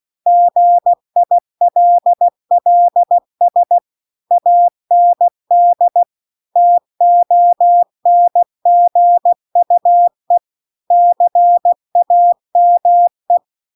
Morse code, heard as a keyed tone, about 16 wpm, 700 Hz. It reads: GILLS AND TONGUE CAME
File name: CW-The-Big-Twoheaded-River-2-16WPM